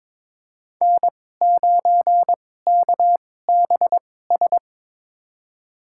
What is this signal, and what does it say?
Morse code, heard as a keyed tone, about 22 wpm, spaced as for 16 wpm, 700 Hz. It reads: N9KBS
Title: USA Call Signs - 2x Character Spacing - 22wpm